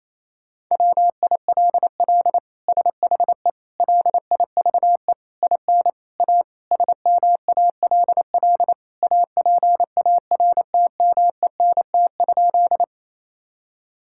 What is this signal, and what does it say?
Morse code, heard as a keyed tone, about 27 wpm, 700 Hz. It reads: WILL SHE LIVE IN A SMALL APARTMENT?
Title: Sentences from Top 1000 Words - Review Part 6 - 28wpm